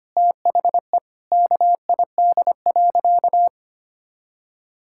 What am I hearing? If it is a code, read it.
Morse code, heard as a keyed tone, about 25 wpm, 700 Hz. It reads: THE KID.
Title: Sentences from Top 1600 Words 25wpm